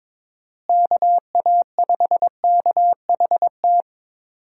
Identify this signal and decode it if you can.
Morse code, heard as a keyed tone, about 22 wpm, 700 Hz. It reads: KA5KHT
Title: USA Call Signs - 6 Characters - Repeated 3x - 22wpm